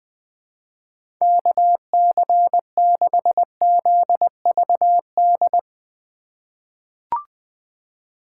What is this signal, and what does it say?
Morse code, heard as a keyed tone, about 20 wpm, 700 Hz. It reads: KC6ZVD
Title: USA Call Signs - 6 Characters - 2x Character Spacing - 20wpm